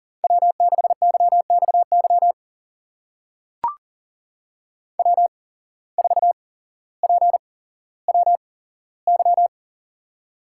Morse code, heard as a keyed tone, about 40 wpm, spaced as for 17 wpm, 700 Hz. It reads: W6YXY W4PWY
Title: USA Call Signs - 5 Characters - 8x Character Spacing - 40wpm